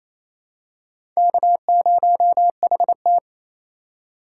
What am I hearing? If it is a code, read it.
Morse code, heard as a keyed tone, about 28 wpm, 700 Hz. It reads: K0HT
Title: K1USN SST QSO Practice - 28wpm